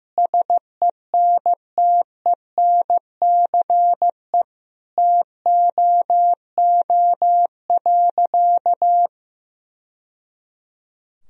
Morse code, heard as a keyed tone, about 16 wpm, 700 Hz. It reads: SENTENCE TOO.
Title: Sentences from Top 1000 Words - Review Part 1 - 15wpm